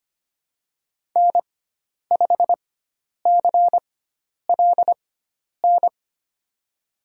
Morse code, heard as a keyed tone, about 25 wpm, spaced as for 10 wpm, 700 Hz. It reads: N5CLN
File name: Lesson 16g - Letter c - Call Signs 25wpm